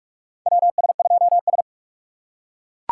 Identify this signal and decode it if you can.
Morse code, heard as a keyed tone, about 41 wpm, 700 Hz. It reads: WS2S
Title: USA Call Signs - 4 Characters - 2x Character Spacing - 45wpm